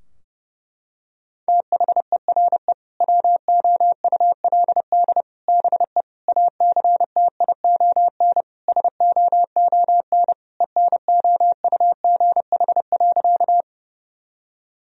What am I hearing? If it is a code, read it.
Morse code, heard as a keyed tone, about 30 wpm, 700 Hz. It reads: THERE WOULD BE ACTION SOON ENOUGH.
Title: Sentences from Top 1000 Words - Review Part 4 - 30wpm